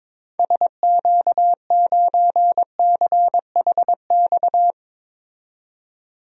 Morse code, heard as a keyed tone, about 22 wpm, 700 Hz. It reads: SQ9CHX
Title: DXCC-100 Call Signs - 6 Characters - 22wpm